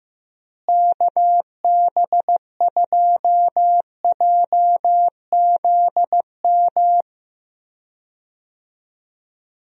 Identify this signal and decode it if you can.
Morse code, heard as a keyed tone, about 15 wpm, 700 Hz. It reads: KB2JZM
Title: USA Call Signs - 6 Characters - Repeated 3x - 15wpm